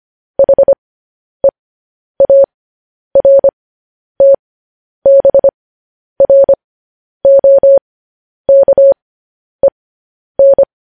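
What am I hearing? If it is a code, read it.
Morse code, heard as a keyed tone, about 25 wpm, spaced as for 10 wpm, 550 Hz. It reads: HEARTBROKEN
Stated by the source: Morse news headlines at 10.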